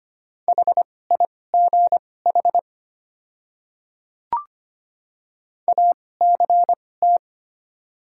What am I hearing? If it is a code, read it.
Morse code, heard as a keyed tone, about 25 wpm, spaced as for 18 wpm, 700 Hz. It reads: HIGH ACT
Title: 2x Character Spacing - Top 200 Words - 25wpm